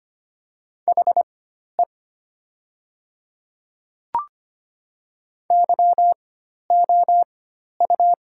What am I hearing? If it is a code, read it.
Morse code, heard as a keyed tone, about 25 wpm, spaced as for 12 wpm, 700 Hz. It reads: HE YOU
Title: 4x Character Spacing - Top 200 Words - 25wpm